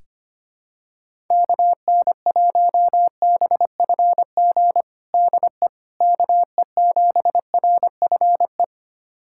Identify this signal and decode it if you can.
Morse code, heard as a keyed tone, about 25 wpm, 700 Hz. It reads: KN1BFG DE KE7RFE
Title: USA Callsign DE Callsign - 6 Characters - 25wpm